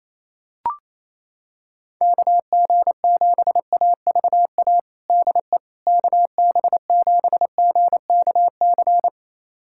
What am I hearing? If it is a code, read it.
Morse code, heard as a keyed tone, about 28 wpm, 700 Hz. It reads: KG7AVA DE KB7GKC
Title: USA Callsign DE Callsign - 6 Characters - 28wpm